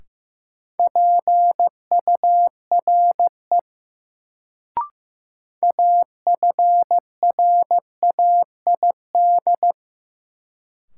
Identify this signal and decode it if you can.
Morse code, heard as a keyed tone, about 16 wpm, 700 Hz. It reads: PURE AFRAID
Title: Sentences from Top 1500 Words - New Word Drill Alt - 15wpm